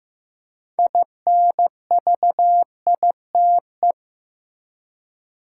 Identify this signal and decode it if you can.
Morse code, heard as a keyed tone, about 15 wpm, 700 Hz. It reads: INVITE
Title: Six Letter Words - 15wpm